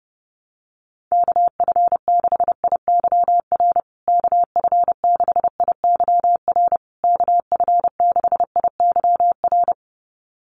Morse code, heard as a keyed tone, about 30 wpm, 700 Hz. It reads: KF6IYR KF6IYR KF6IYR
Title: USA Call Signs Repeated 3x 30wpm